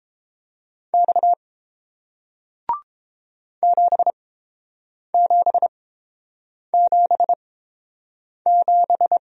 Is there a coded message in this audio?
Yes: Morse code from 0.9 to 9.2 s, heard as a keyed tone, about 30 wpm, spaced as for 7 wpm, 700 Hz. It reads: X 7777